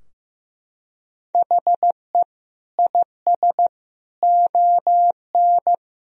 Morse code, heard as a keyed tone, about 15 wpm, 700 Hz. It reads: HE IS ON
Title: Sentences from Top 1000 Words - Review Part 4 - 15wpm